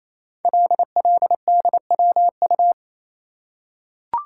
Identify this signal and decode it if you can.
Morse code, heard as a keyed tone, about 27 wpm, 700 Hz. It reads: LLDWU
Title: Code Groups - 5 Letter:Numbers - 28wpm